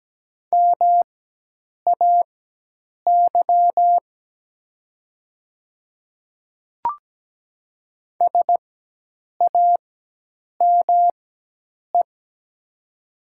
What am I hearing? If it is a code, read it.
Morse code, heard as a keyed tone, about 17 wpm, spaced as for 8 wpm, 700 Hz. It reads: MAY SAME
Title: 4x Character Spacing - Top 300 Words - 17wpm